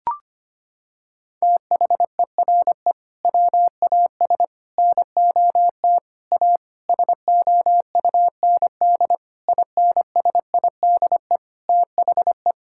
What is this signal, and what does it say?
Morse code, heard as a keyed tone, about 25 wpm, 700 Hz. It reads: THERE WAS NOT A SOUND INSIDE THE
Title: Sentences from Top 1000 Words - Review Part 6 - 25wpm